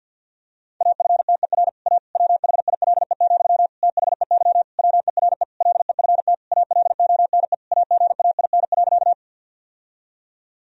Morse code, heard as a keyed tone, about 50 wpm, 700 Hz. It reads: AFTER A WHILE, THEY WERE LEFT ALONE AGAIN.
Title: Sentences from Top 1000 Words - Review Part 1 - 50wpm